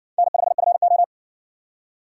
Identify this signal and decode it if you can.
Morse code, heard as a keyed tone, about 58 wpm, 700 Hz. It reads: N5FO